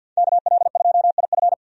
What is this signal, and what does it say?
Morse code, heard as a keyed tone, about 53 wpm, 700 Hz. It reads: KL2IF
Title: USA Call Signs - 5 Characters - 50wpm